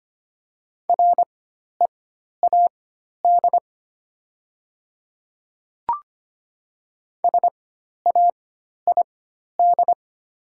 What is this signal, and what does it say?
Morse code, heard as a keyed tone, about 25 wpm, spaced as for 12 wpm, 690 Hz. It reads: READ SAID
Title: 4x Character Spacing - Top 200 Words - 25wpm